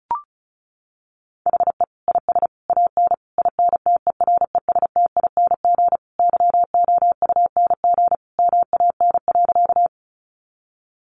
Morse code, heard as a keyed tone, about 34 wpm, 700 Hz. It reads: HE IS AN INTERESTING YOUNG MAN.
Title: Sentences from Top 1000 Words 35wpm